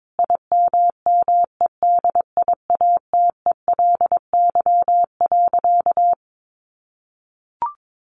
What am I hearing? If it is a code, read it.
Morse code, heard as a keyed tone, about 22 wpm, 700 Hz. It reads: IMMEDIATELY.
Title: Sentences from Top 1100 Words 22wpm